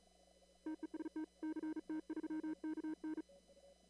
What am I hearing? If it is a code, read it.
Morse code, heard as a keyed tone, about 37 wpm, 310 Hz. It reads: TEST CT3KN
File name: CW / kt5j wpx cw-op-k5pi